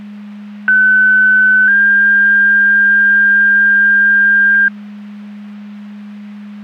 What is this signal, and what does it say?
Two-tone page, heard as a one-second tone, then a three-second tone, about 1500 then 1600 Hz. It is heard from 0.7 to 4.7 s.
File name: Portage / 01755 11-02 Portage 154.385 Paging